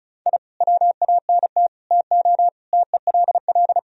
Morse code, heard as a keyed tone, about 36 wpm, 700 Hz. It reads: I WANT TO TELL
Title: Sentences from Top 1000 Words - Review Part 4 - 35wpm